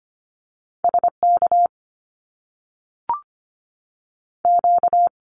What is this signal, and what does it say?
Morse code, heard as a keyed tone, about 25 wpm, 700 Hz. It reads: SK Q
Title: Canadian Provinces and Territories 25wpm